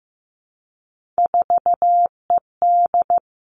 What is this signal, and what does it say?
Morse code, heard as a keyed tone, about 15 wpm, 700 Hz. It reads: 4ED
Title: Three Challenging Letter-Number Combinations 15wpm